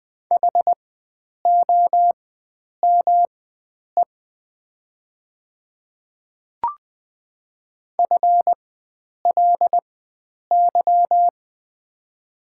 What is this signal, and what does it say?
Morse code, heard as a keyed tone, about 21 wpm, spaced as for 9 wpm, 700 Hz. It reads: HOME FLY
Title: 4x Character Spacing - Top 500 Words - 20wpm